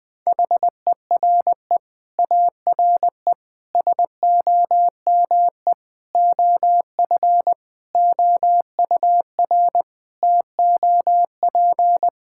Morse code, heard as a keyed tone, about 21 wpm, 700 Hz. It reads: HERE ARE SOME OF OUR TOP
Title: Sets of 9 Words - 20wpm